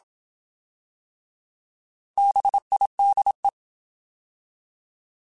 Morse code, heard as a keyed tone, about 27 wpm, 800 Hz. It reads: BIDE
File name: Alphabetical / Alphabetical mp3s / Bs @25